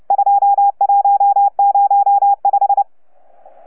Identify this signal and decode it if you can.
Morse code, heard as a keyed tone, about 29 wpm, 810 Hz. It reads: 2105
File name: UP2L-20-CW